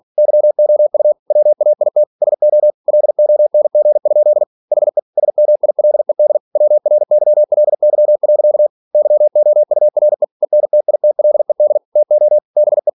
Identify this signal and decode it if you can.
Morse code, heard as a keyed tone, about 46 wpm, 600 Hz. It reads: YOU WAIT SO LONG? HE SMILED WRYLY. YOURE ENTITLED TO BE
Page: Never 47wpm